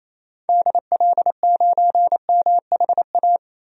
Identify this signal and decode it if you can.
Morse code, heard as a keyed tone, about 28 wpm, 700 Hz. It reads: DL9MHA
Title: DXCC-100 Call Signs - 6 Characters - 28wpm